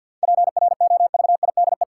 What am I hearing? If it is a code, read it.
Morse code, heard as a keyed tone, about 50 wpm, 700 Hz. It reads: PROVIDE
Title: Seven Letter Words - 50wpm